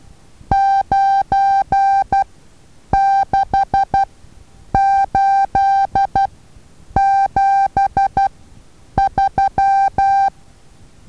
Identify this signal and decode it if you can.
Morse code, heard as a keyed tone, about 12 wpm, spaced as for 8 wpm, 790 Hz. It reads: 96873
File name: Taviro HA6VA A